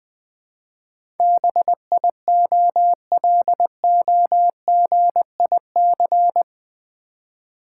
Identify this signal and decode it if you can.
Morse code, heard as a keyed tone, about 20 wpm, 700 Hz. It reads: BIOLOGIC
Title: Build A Word - 4 Letters At A Time - 10-Letter Words - 20wpm